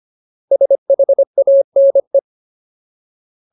Morse code, heard as a keyed tone, about 25 wpm, 550 Hz. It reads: SHANE
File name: Ultimate Word List / 25WPM Ultimate